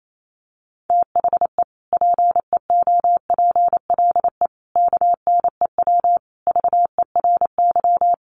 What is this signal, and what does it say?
Morse code, heard as a keyed tone, about 28 wpm, 700 Hz. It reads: THE PEOPLE KNEW VERY
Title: Sentences from Top 1100 Words 28wpm